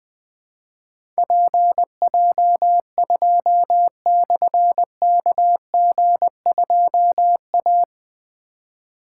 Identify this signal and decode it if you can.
Morse code, heard as a keyed tone, about 20 wpm, 700 Hz. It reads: PJ2/KG2A
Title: Difficult Call Signs 20wpm